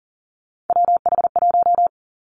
Morse code, heard as a keyed tone, about 41 wpm, 690 Hz. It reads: WH1
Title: Three Challenging Letter-Number Combinations 40wpm